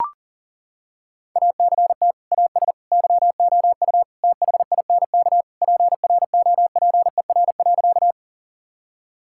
Morse code, heard as a keyed tone, about 40 wpm, 700 Hz. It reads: ACT AS YOU THINK PROPER.
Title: Sentences from Top 1000 Words - Review Part 5 - 40wpm